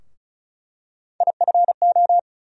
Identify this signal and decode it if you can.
Morse code, heard as a keyed tone, about 34 wpm, 700 Hz. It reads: IFO
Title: USA Call Sign Postfixes - 3 Letters 35wpm